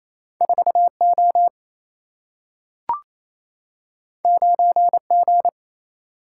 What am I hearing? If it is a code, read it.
Morse code, heard as a keyed tone, about 28 wpm, 700 Hz. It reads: 4 O 9 G
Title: Single-Letter Single-Number 28wpm